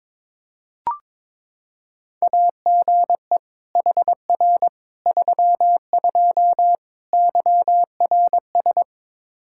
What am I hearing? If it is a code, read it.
Morse code, heard as a keyed tone, about 22 wpm, 700 Hz. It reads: AGE HR 32 YRS
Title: Practice QSO - Exchange 3 - More Info - 22wpm